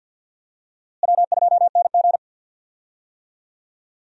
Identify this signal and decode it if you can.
Morse code, heard as a keyed tone, about 52 wpm, 700 Hz. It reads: W2NG
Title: USA Call Signs - 4 Characters - Repeated 3x - 50wpm